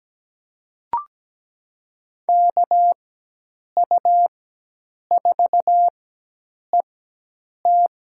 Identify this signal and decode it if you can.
Morse code, heard as a keyed tone, about 17 wpm, spaced as for 8 wpm, 700 Hz. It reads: KU4ET
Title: USA Call Signs - 5 Characters - 4x Character Spacing - 17wpm